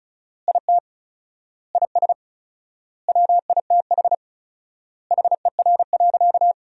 Morse code, heard as a keyed tone, about 34 wpm, 700 Hz. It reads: IT IS WITH HER.
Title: 4x Word-Spacing - Sentences from Top 100 Words - 35wpm